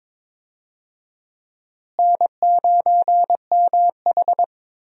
Morse code, heard as a keyed tone, about 21 wpm, 700 Hz. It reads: N9MH
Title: USA Call Signs - 4 Characters - Repeated 3x - 22wpm